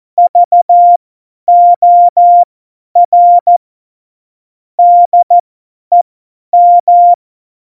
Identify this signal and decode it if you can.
Morse code, heard as a keyed tone, about 14 wpm, spaced as for 10 wpm, 700 Hz. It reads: VOR DEM
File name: Morsekurs / Witze / Witz 03. November 2020, 50 F70